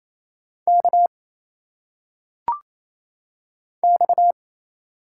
Morse code, heard as a keyed tone, about 27 wpm, 700 Hz. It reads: K X
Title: Single Letters - Phonetic Pronunciation 28wpm